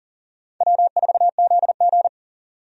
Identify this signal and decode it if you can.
Morse code, heard as a keyed tone, about 37 wpm, 700 Hz. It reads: W4ZG
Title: USA Call Signs - 4 Characters - 40wpm